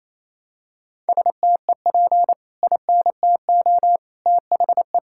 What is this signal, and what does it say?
Morse code, heard as a keyed tone, about 28 wpm, 700 Hz. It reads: STEP INTO THE
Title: Sentences from Top 1600 Words 28wpm